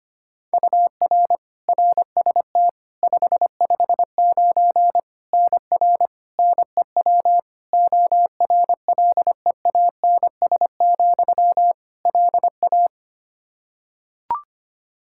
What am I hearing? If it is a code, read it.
Morse code, heard as a keyed tone, about 25 wpm, 700 Hz. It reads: UR RST 559 NR NEW ORLEANS, LA
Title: K1USN SST QSO Practice 25wpm